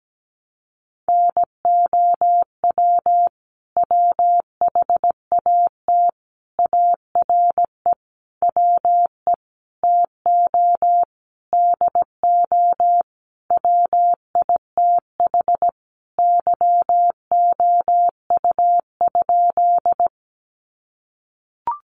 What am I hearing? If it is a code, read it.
Morse code, heard as a keyed tone, about 17 wpm, 700 Hz. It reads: NOW WHAT ARE WE TO DO WITH YOU?
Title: Sentences from Top 100 Words 17wpm